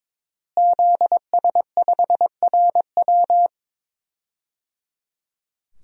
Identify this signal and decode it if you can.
Morse code, heard as a keyed tone, about 22 wpm, 700 Hz. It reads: ZS5RW
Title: DXCC-100 Call Signs - 5 Characters - Repeated 3x - 22wpm